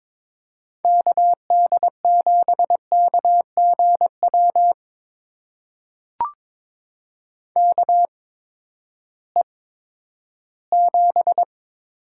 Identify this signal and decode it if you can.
Morse code, heard as a keyed tone, about 21 wpm, spaced as for 12 wpm, 700 Hz. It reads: KD7KGW KE7
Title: USA Call Signs - 6 Characters - 8x Character Spacing - 22wpm